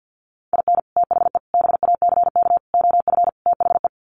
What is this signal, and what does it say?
Morse code, heard as a keyed tone, about 50 wpm, 700 Hz. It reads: IN THE BACK OF THE
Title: Sets of 5 Words 50wpm